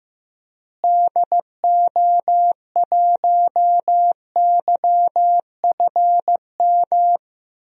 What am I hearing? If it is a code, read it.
Morse code, heard as a keyed tone, about 15 wpm, 700 Hz. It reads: DO1YFM
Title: DXCC-100 Call Signs - 6 Characters - Repeated 2x - 15wpm